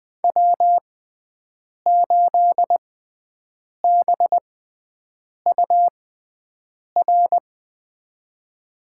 Morse code, heard as a keyed tone, about 20 wpm, spaced as for 7 wpm, 700 Hz. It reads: W8BUR
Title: USA Call Signs - 6x Character Spacing - 20wpm